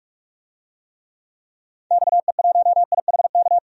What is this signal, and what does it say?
Morse code, heard as a keyed tone, about 42 wpm, 700 Hz. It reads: XE1IHK
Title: DXCC-100 Call Signs - 6 Characters - 45wpm